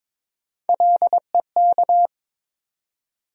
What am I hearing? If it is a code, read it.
Morse code, heard as a keyed tone, about 21 wpm, 700 Hz. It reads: LEK
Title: USA Call Sign Postfixes - 3 Letters 22wpm